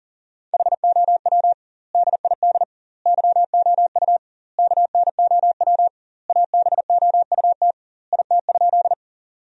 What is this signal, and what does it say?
Morse code, heard as a keyed tone, about 40 wpm, 700 Hz. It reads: HOW DID YOU KNOW ABOUT IT?
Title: 2x Word-Spacing - Sentences from Top 100 Words - 40wpm